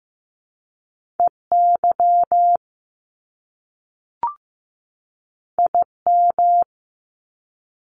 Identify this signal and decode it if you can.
Morse code, heard as a keyed tone, about 15 wpm, 700 Hz. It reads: EY IM
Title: Top Two Letter Combinations in Words 15wpm